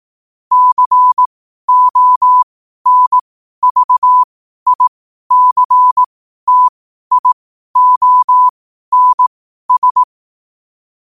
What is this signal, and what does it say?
Morse code, heard as a keyed tone, about 18 wpm, spaced as for 12 wpm, 1000 Hz. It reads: CONVICTIONS
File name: 12wpm